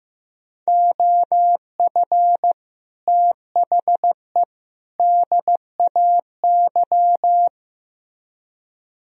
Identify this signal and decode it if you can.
Morse code, heard as a keyed tone, about 16 wpm, 700 Hz. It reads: OF THE DAY
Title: Speed-Racing - Sets of 3 Words 15wpm